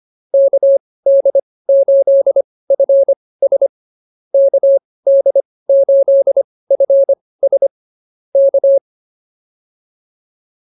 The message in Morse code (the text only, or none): KD8FS KD8FS K